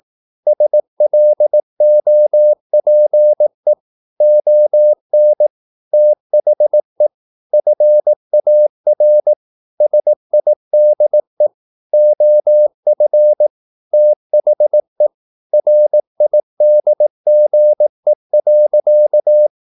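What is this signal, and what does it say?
Morse code, heard as a keyed tone, about 18 wpm, 600 Hz. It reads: SLOPE ON THE FAR SIDE OF THE RIDGE.
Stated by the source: Morse practice at 18 wpm, English text, from CW med Gnister.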